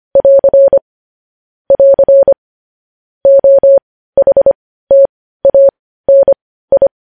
Morse code, heard as a keyed tone, about 25 wpm, spaced as for 15 wpm, 550 Hz. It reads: <AR> <AR> OHTANI